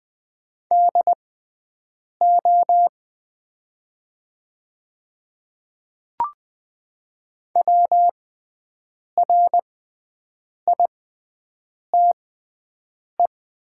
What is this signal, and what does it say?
Morse code, heard as a keyed tone, about 20 wpm, spaced as for 7 wpm, 700 Hz. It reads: DO WRITE
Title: 6x Character Spacing - Top 100 Words - 20wpm